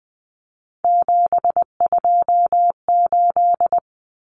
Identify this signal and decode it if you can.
Morse code, heard as a keyed tone, about 20 wpm, 700 Hz. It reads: 728
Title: Common Cut Numbers - 3-Digit 20wpm